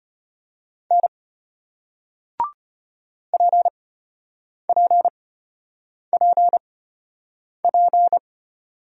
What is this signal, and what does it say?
Morse code, heard as a keyed tone, about 33 wpm, spaced as for 8 wpm, 690 Hz. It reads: N PPPP